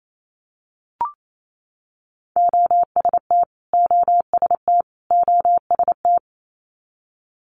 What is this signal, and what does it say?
Morse code, heard as a keyed tone, about 28 wpm, 700 Hz. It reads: OST OST OST
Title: Multi-letter Phonemes Repeated 3x 28wpm